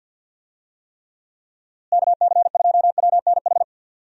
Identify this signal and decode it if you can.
Morse code, heard as a keyed tone, about 48 wpm, 700 Hz. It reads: KC2WNH
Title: USA Call Signs - 6 Characters - Repeated 3x - 50wpm